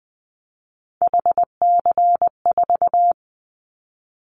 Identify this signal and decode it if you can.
Morse code, heard as a keyed tone, about 20 wpm, 700 Hz. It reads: HC4
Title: Callsign Prefixes for Top 100 DXCC - 20wpm